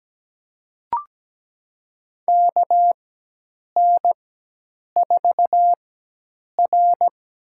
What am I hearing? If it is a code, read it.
Morse code, heard as a keyed tone, about 17 wpm, spaced as for 8 wpm, 700 Hz. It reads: KN4R
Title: USA Call Signs - 4x Character Spacing - 17wpm